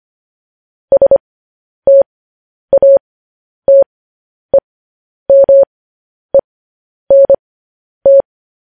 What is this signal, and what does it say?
Morse code, heard as a keyed tone, about 25 wpm, spaced as for 10 wpm, 550 Hz. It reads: STATEMENT
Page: news-headlines-in-morse-code-at-10-wpm